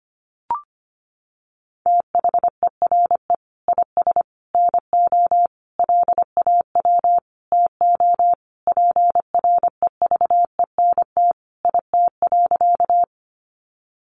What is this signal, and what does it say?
Morse code, heard as a keyed tone, about 25 wpm, 700 Hz. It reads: THERE IS NO LAW TO PREVENT IT.
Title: Sentences from Top 1100 Words 25wpm